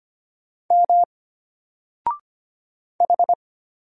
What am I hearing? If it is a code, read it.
Morse code, heard as a keyed tone, about 25 wpm, 700 Hz. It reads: M H
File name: Single-Letter-Number-25wpm